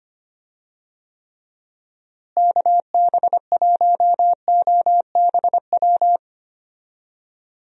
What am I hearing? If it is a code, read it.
Morse code, heard as a keyed tone, about 25 wpm, 700 Hz. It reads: KB1OBW
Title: USA Call Signs - 6 Characters - Repeated 3x - 25wpm